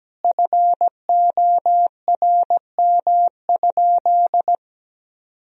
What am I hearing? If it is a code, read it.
Morse code, heard as a keyed tone, about 17 wpm, 700 Hz. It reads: FORM?
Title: Sentences from Top 1000 Words - Review Part 1 - 17wpm